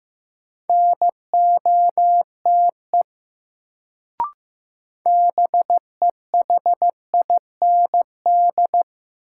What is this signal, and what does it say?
Morse code, heard as a keyed tone, about 15 wpm, 700 Hz. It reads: NOTE BEHIND
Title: Sentences from Top 500 Words - New Word Drill - 15wpm